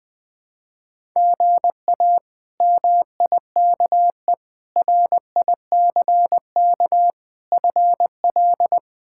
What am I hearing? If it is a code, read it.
Morse code, heard as a keyed tone, about 21 wpm, 700 Hz. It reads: GA MIKE RICK FL
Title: K1USN SST QSO Practice - 20wpm